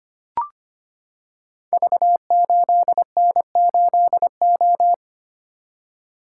Morse code, heard as a keyed tone, about 25 wpm, 700 Hz. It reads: V8N8O
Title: Code Groups - 5 Letter:Numbers - 25wpm